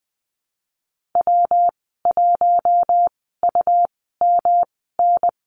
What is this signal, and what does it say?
Morse code, heard as a keyed tone, about 19 wpm, spaced as for 14 wpm, 700 Hz. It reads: W1UMN
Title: 186 Lesson 30g - Letter b - Call Signs 20wpm